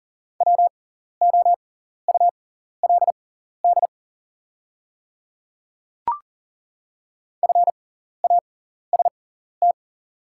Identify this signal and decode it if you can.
Morse code, heard as a keyed tone, about 38 wpm, spaced as for 14 wpm, 700 Hz. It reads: WOULD FAST
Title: 6x Character Spacing - Top 500 Words - 40wpm